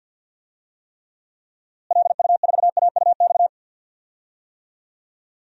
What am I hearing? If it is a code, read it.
Morse code, heard as a keyed tone, about 46 wpm, 700 Hz. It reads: LU4AUX
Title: DXCC-100 Call Signs - 6 Characters - Repeated 2x - 50wpm